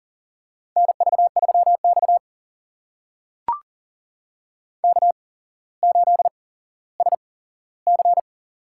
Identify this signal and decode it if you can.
Morse code, heard as a keyed tone, about 38 wpm, spaced as for 17 wpm, 690 Hz. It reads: NV3X K8SC
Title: USA Call Signs - 4 Characters - 8x Character Spacing - 40wpm